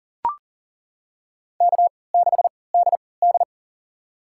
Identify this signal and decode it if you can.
Morse code, heard as a keyed tone, about 40 wpm, spaced as for 23 wpm, 700 Hz. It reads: K6DD
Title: USA Call Signs - 4 Characters - 3x Character Spacing - 40wpm